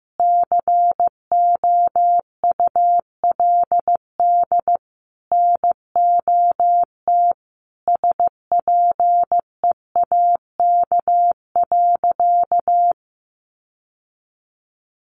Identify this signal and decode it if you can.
Morse code, heard as a keyed tone, about 15 wpm, 700 Hz. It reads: COULD NOT SPEAK.
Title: Sentences from Top 1100 Words 15wpm